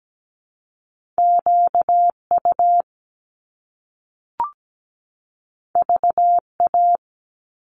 Morse code, heard as a keyed tone, about 17 wpm, 700 Hz. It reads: QU VA
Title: Top Two Letter Combinations in Words 17wpm